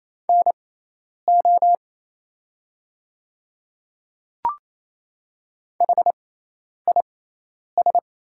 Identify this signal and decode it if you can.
Morse code, heard as a keyed tone, about 28 wpm, spaced as for 10 wpm, 700 Hz. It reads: NO HIS
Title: 6x Character Spacing - Top 100 Words - 28wpm